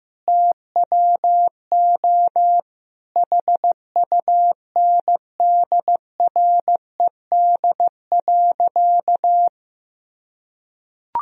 Morse code, heard as a keyed tone, about 15 wpm, 700 Hz. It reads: TWO HUNDRED.
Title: Sentences from Top 1000 Words - Review Part 2 - 15wpm